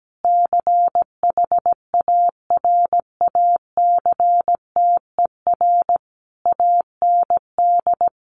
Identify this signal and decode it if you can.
Morse code, heard as a keyed tone, about 17 wpm, 700 Hz. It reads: CHARACTER AND
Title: Sentences from Top 1200 Words 17wpm